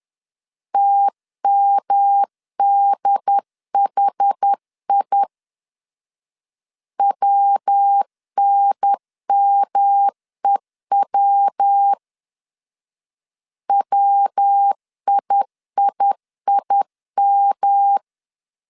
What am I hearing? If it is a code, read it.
Morse code, heard as a keyed tone, about 10 wpm, 800 Hz. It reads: TMDHI WNMEW WIIIM
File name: CW / DET / 1 3 1500C S C 10wpm 3335